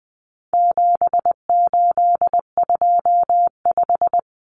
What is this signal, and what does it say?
Morse code, heard as a keyed tone, about 20 wpm, 700 Hz. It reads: 7825
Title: Numbers - 4-Digit 20wpm